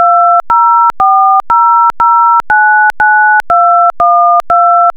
DTMF, heard as telephone touch tones, tones of about 430 ms. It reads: 2040099212